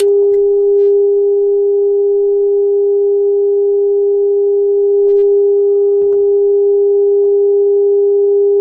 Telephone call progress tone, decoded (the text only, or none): dial tone